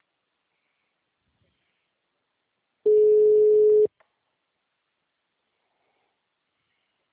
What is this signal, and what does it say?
Ringback tone (Europe, 425 Hz), with speech.